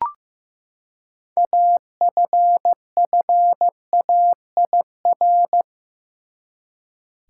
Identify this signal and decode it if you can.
Morse code, heard as a keyed tone, about 15 wpm, 700 Hz. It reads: AFFAIR
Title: Six Letter Words - 15wpm